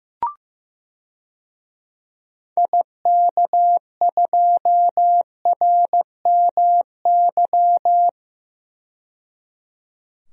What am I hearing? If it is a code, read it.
Morse code, heard as a keyed tone, about 15 wpm, 700 Hz. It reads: IK2RMY
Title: DXCC-100 Call Signs - 6 Characters - 15wpm